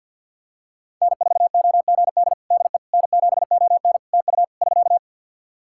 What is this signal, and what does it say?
Morse code, heard as a keyed tone, about 52 wpm, 700 Hz. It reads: N4OGD DE N7ON TU <SK>
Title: Practice QSO - Exchange 4 - Ending - 50wpm